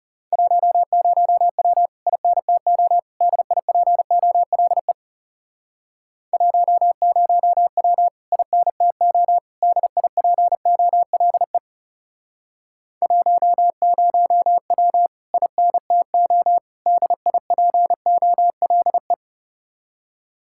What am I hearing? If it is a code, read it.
Morse code, heard as a keyed tone, about 38 wpm, 700 Hz. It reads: 10W INTO DIPOLE 10W INTO DIPOLE 10W INTO DIPOLE